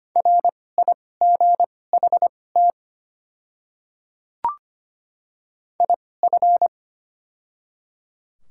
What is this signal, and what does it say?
Morse code, heard as a keyed tone, about 25 wpm, spaced as for 18 wpm, 700 Hz. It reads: RIGHT IF
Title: 2x Character Spacing - Top 200 Words - 25wpm